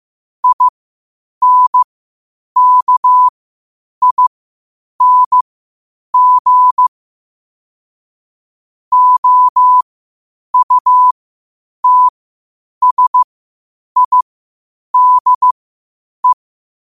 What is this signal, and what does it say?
Morse code, heard as a keyed tone, about 15 wpm, spaced as for 8 wpm, 1000 Hz. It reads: INKING OUTSIDE
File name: qotdfiles / 8wpm